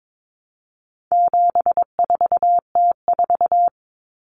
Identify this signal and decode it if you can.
Morse code, heard as a keyed tone, about 22 wpm, 700 Hz. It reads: 74T4
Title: Common Cut Numbers - 4-Digit 22wpm